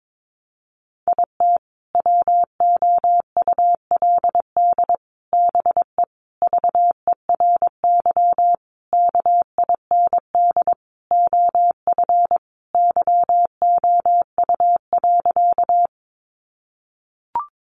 Morse code, heard as a keyed tone, about 22 wpm, 700 Hz. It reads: IT WOULD BE VERY KIND OF YOU.
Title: Sentences from Top 200 Words 22wpm